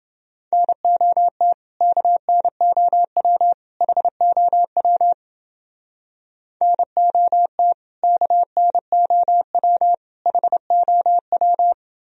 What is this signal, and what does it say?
Morse code, heard as a keyed tone, about 28 wpm, 700 Hz. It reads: NOT KNOW HOW NOT KNOW HOW